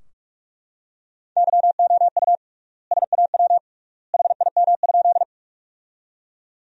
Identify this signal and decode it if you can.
Morse code, heard as a keyed tone, about 42 wpm, 700 Hz. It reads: YOU SAW HIM?
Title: 3x Word-Spacing - Sentences from Top 300 Words - 45wpm